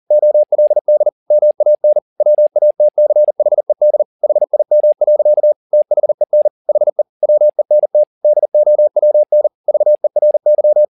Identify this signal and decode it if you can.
Morse code, heard as a keyed tone, about 40 wpm, 600 Hz. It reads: OLD MAN WATCHED HIM. THEN HE WENT DOWN VERY
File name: the old man and the sea 2 40wpm